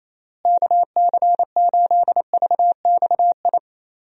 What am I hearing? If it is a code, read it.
Morse code, heard as a keyed tone, about 27 wpm, 700 Hz. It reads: KC8VXI
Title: USA Call Signs - 6 Characters - Repeated 3x - 28wpm